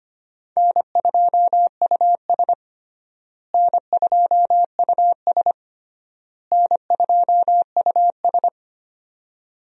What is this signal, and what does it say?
Morse code, heard as a keyed tone, about 25 wpm, 700 Hz. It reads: N2US N2US N2US
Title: USA Call Signs - 4 Characters - Repeated 3x - 25wpm